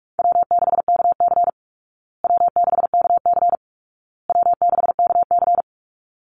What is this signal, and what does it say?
Morse code, heard as a keyed tone, about 44 wpm, 700 Hz. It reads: W6KC W6KC W6KC
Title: Easy Call Signs Repeated 3x - 45wpm